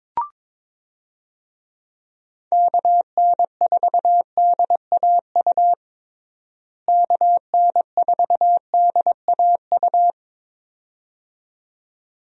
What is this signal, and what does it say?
Morse code, heard as a keyed tone, about 21 wpm, 700 Hz. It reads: KN4DAU KN4DAU
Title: USA Call Signs - 6 Characters - Repeated 2x - 22wpm